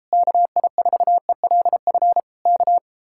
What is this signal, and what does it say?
Morse code, heard as a keyed tone, about 33 wpm, 700 Hz. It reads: KI4ELF K